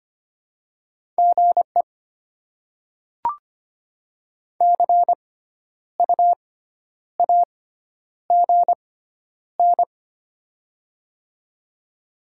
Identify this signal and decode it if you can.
Morse code, heard as a keyed tone, about 25 wpm, spaced as for 10 wpm, 700 Hz. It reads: GE CU AGN
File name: Top 20 QSO Elements - 6x Character Spacing - 25wpm